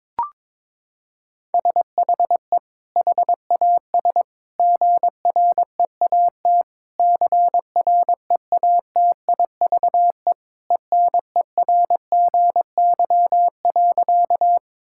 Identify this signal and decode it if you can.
Morse code, heard as a keyed tone, about 22 wpm, 700 Hz. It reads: SHE HAS GREAT CREATIVE ENERGY.
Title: Sentences from Top 1000 Words - Review Part 6 - 22wpm